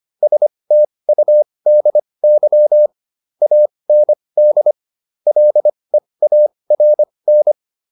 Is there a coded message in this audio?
Yes: Morse code from 0.2 to 7.5 s, heard as a keyed tone, about 25 wpm, spaced as for 20 wpm, 600 Hz. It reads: STUDY AND LEARN